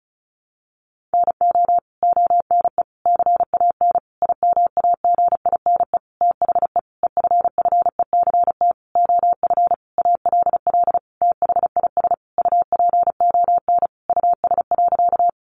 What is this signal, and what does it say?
Morse code, heard as a keyed tone, about 34 wpm, 700 Hz. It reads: NO ONE CAN IMAGINE THE EFFECT OF ALL THIS UPON US.
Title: Sentences from Top 1400 Words 35wpm